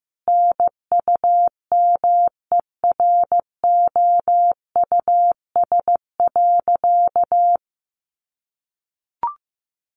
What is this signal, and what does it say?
Morse code, heard as a keyed tone, about 15 wpm, 700 Hz. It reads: NUMEROUS.
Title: Sentences from Top 1400 Words 15wpm